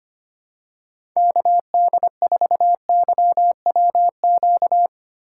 Morse code, heard as a keyed tone, about 25 wpm, 700 Hz. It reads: KD4YWQ
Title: USA Call Signs - 6 Characters - 3x Character Spacing - 25wpm